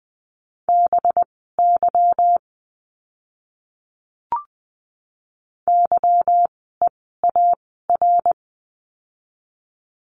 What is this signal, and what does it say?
Morse code, heard as a keyed tone, about 20 wpm, spaced as for 15 wpm, 700 Hz. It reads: BY YEAR
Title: 230 Lesson 37f - Letter x - Words 20wpm